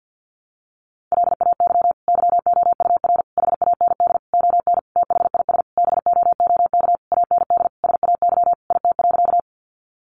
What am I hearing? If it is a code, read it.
Morse code, heard as a keyed tone, about 50 wpm, 700 Hz. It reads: LAY YOUR HAND ON THIS BOOK AND SAY IT.
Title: Sentences from Top 400 Words 50wpm